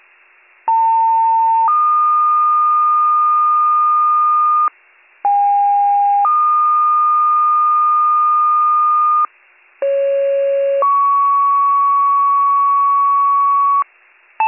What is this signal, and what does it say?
Two-tone page, heard as a one-second tone, then a three-second tone, about 900 then 1220 Hz, about 810 then 1220 Hz, about 550 then 1120 Hz. It is heard from 0.7 to 13.8 s.